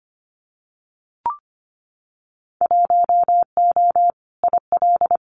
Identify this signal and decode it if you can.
Morse code, heard as a keyed tone, about 25 wpm, 700 Hz. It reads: 1O IL
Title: Winter Field Day Exchanges 25wpm